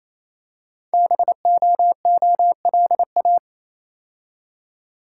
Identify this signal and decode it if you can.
Morse code, heard as a keyed tone, about 29 wpm, 700 Hz. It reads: BOOLA
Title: Code Groups - 5 Letters - 28wpm